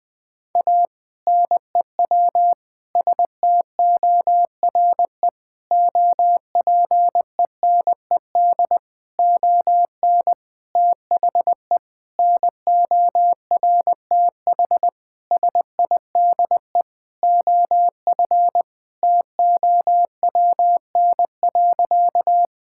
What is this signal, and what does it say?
Morse code, heard as a keyed tone, about 20 wpm, 700 Hz. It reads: A NEW STORE OPENED ON THE NORTH SIDE OF TOWN.
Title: Sentences with 6-Word Phrases - 20wpm